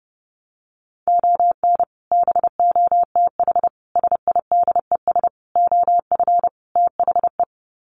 Morse code, heard as a keyed tone, about 28 wpm, 700 Hz. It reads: ON BOTH SIDES OF THE
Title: Sets of 5 Words 30wpm